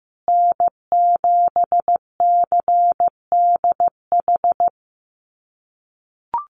Morse code, heard as a keyed tone, about 15 wpm, 700 Hz. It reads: N7CDH
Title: USA Call Signs Repeated 3x 15wpm